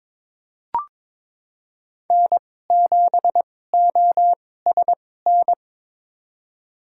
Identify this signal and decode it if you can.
Morse code, heard as a keyed tone, about 22 wpm, spaced as for 16 wpm, 700 Hz. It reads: N7OSN
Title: USA Call Signs - 5 Characters - 2x Character Spacing - 22wpm